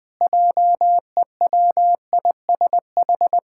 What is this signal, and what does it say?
Morse code, heard as a keyed tone, about 20 wpm, 700 Hz. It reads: JEWISH
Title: Pangram Challenge v2 - 20wpm